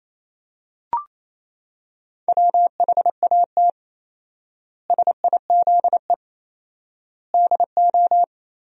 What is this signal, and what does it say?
Morse code, heard as a keyed tone, about 28 wpm, 700 Hz. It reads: WHAT SIZE DO
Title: 4x Word-Spacing - Sentences from Top 400 Words - 28wpm